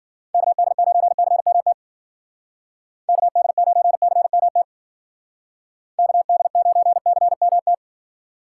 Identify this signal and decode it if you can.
Morse code, heard as a keyed tone, about 49 wpm, 700 Hz. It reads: KD9CMT KD9CMT KD9CMT
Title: Speed Racing - USA 2x3 Call Signs 40wpm